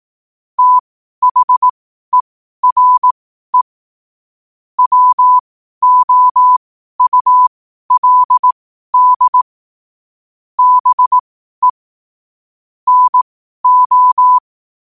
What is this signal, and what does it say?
Morse code, heard as a keyed tone, about 18 wpm, spaced as for 12 wpm, 1000 Hz. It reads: THERE WOULD BE NO
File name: qotdfiles / 12wpm